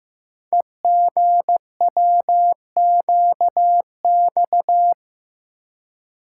Morse code, heard as a keyed tone, about 15 wpm, 700 Hz. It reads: EGWQX
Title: Code Groups - 5 Letters - 15wpm